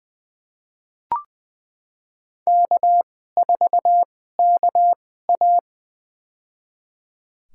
Morse code, heard as a keyed tone, about 20 wpm, spaced as for 14 wpm, 700 Hz. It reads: K4KA
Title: USA Call Signs - 2x Character Spacing - 20wpm